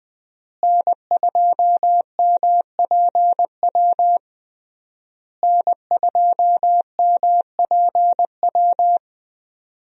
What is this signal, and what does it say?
Morse code, heard as a keyed tone, about 20 wpm, 700 Hz. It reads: N2MPW N2MPW
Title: USA Call Signs - 5 Characters - Repeated 3x - 20wpm